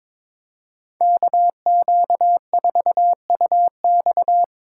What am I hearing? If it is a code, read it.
Morse code, heard as a keyed tone, about 26 wpm, 700 Hz. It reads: KQ4UX
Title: Speed Racing - USA 2x2 Call Signs 22wpm